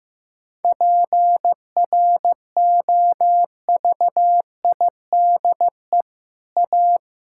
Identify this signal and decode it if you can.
Morse code, heard as a keyed tone, about 15 wpm, 700 Hz. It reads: PROVIDE A
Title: Sentences from Top 1000 Words - Review Part 5 - 15wpm